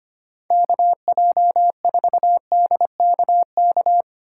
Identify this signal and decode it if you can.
Morse code, heard as a keyed tone, about 30 wpm, 700 Hz. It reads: KJ4DKK
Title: Speed Racing - USA 2x3 Call Signs 25wpm